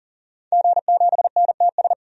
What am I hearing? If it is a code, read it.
Morse code, heard as a keyed tone, about 40 wpm, 700 Hz. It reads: G7NTS
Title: DXCC-100 Call Signs - 5 Characters - 40wpm